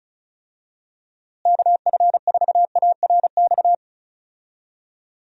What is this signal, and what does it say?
Morse code, heard as a keyed tone, about 36 wpm, 700 Hz. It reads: KF4ARX
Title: USA Call Signs - 6 Characters - Repeated 2x - 35wpm